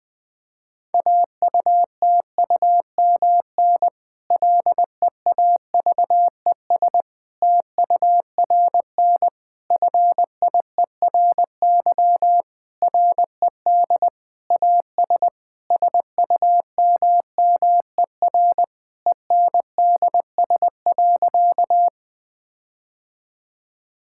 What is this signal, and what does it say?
Morse code, heard as a keyed tone, about 20 wpm, 700 Hz. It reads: AUTUMN LEAVES TURN FIERY RED AS SUMMER ENDS.
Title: Difficult Sentences with Hints - 20wpm